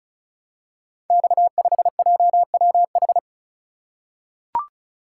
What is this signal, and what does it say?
Morse code, heard as a keyed tone, about 36 wpm, 700 Hz. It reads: X5JWH
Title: Code Groups - 5 Letter:Numbers - 35wpm